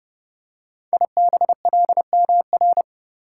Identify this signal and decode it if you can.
Morse code, heard as a keyed tone, about 30 wpm, 700 Hz. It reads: IBLMR
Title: Code Groups - 5 Letters - 30wpm